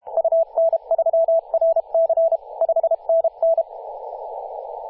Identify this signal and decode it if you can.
Morse code, heard as a keyed tone, about 32 wpm, 660 Hz, with radio noise. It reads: UN3RC 5NN
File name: posts / 2016-oct-05-2124z-t31t-18-072mhz-cw